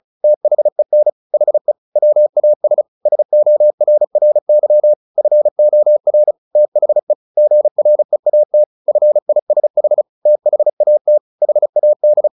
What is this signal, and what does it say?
Morse code, heard as a keyed tone, about 34 wpm, 600 Hz. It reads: THEN HE WAS SORRY FOR THE GREAT FISH THAT HAD
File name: the old man and the sea 2 35wpm